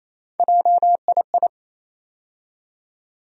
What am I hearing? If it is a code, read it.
Morse code, heard as a keyed tone, about 28 wpm, 700 Hz. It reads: JII
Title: USA Call Sign Postfixes - 3 Letters 28wpm